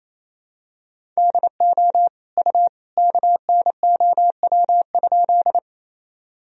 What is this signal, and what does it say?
Morse code, heard as a keyed tone, about 28 wpm, 700 Hz. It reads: DO U KNOW?
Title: Sentences from Top 100 Words and Abbreviations 28wpm